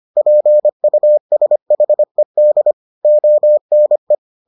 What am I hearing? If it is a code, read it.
Morse code, heard as a keyed tone, about 25 wpm, 600 Hz. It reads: PUSHED ONE